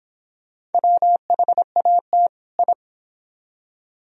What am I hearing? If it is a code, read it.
Morse code, heard as a keyed tone, about 29 wpm, 700 Hz. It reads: WHAT I